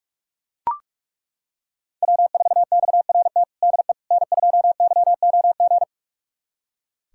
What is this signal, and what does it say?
Morse code, heard as a keyed tone, about 41 wpm, 690 Hz. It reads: W4XRT DE N2YOG
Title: USA Callsign DE Callsign 45wpm